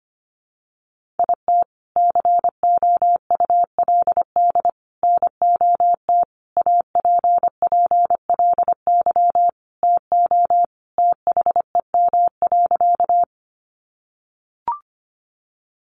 Morse code, heard as a keyed tone, about 25 wpm, 700 Hz. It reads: IT COULD NOT APPLY TO THEM.
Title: Sentences from Top 1200 Words 25wpm